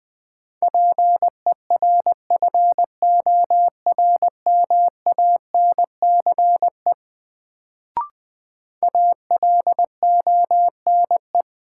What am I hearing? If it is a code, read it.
Morse code, heard as a keyed tone, about 20 wpm, 700 Hz. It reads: PERFORMANCE ALONE
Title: Sentences from Top 1000 Words - New Word Drill - 20wpm